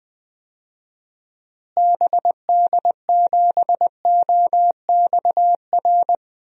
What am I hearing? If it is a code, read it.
Morse code, heard as a keyed tone, about 20 wpm, 700 Hz. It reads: BD7OXR
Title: DXCC-100 Call Signs - 6 Characters - Repeated 2x - 20wpm